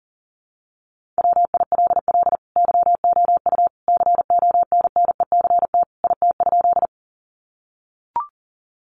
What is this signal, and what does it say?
Morse code, heard as a keyed tone, about 40 wpm, 700 Hz. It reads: WILL YOU CONNECT IT?
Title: Sentences from Top 1500 Words 40wpm